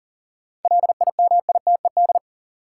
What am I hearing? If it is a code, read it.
Morse code, heard as a keyed tone, about 40 wpm, 700 Hz. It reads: LIMITED
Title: Sentences from Top 900 Words - New Word Drill - 40wpm